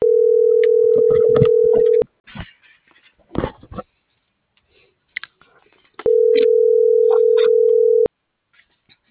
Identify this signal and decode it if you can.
Ringback tone (North America), 440 + 480 Hz.